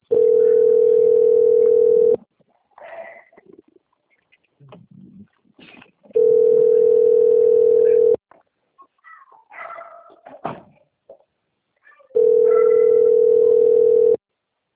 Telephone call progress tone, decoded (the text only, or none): ringback tone